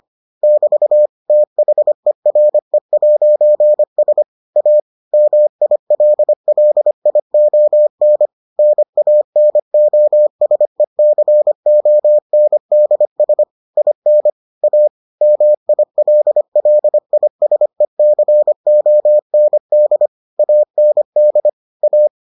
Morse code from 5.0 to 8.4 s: MILLION

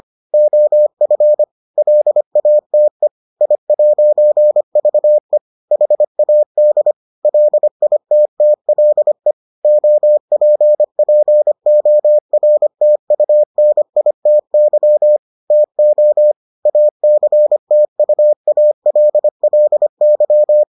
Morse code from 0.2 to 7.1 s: OF LATE I'VE HAD

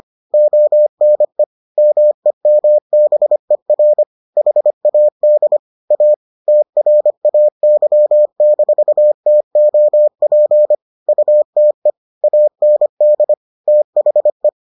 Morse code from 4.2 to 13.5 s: HAD A TRAY-TOP UTE AND

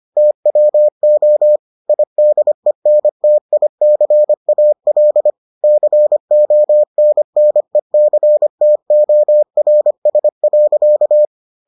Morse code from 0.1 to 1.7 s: TWO

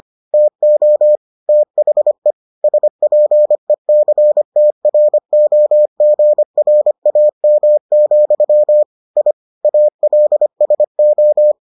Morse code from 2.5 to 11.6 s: SPECTROGRAM, I ALSO

